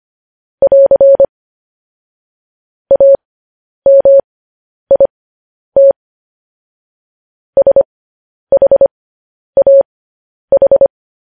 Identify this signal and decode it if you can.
Morse code, heard as a keyed tone, about 25 wpm, spaced as for 10 wpm, 550 Hz. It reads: <AR> AMIT SHAH